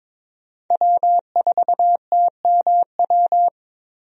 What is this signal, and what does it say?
Morse code, heard as a keyed tone, about 22 wpm, 700 Hz. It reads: W4TMW
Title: USA Call Signs - 5 Characters - Repeated 2x - 22wpm